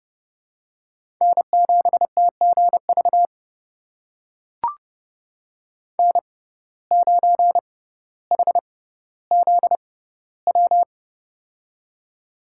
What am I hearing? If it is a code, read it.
Morse code, heard as a keyed tone, about 30 wpm, spaced as for 15 wpm, 700 Hz. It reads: N7TGV N9HZW
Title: USA Call Signs - 5 Characters - 6x Character Spacing - 30wpm